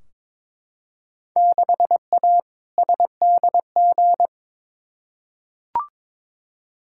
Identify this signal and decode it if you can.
Morse code, heard as a keyed tone, about 22 wpm, 700 Hz. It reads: 6A SDG
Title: Field Day Exchanges Encore - 22wpm